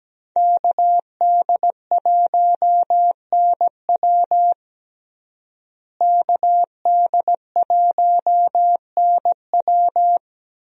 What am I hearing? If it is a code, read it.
Morse code, heard as a keyed tone, about 17 wpm, 700 Hz. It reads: KD1NW KD1NW
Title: USA Call Signs - 5 Characters - Repeated 3x - 17wpm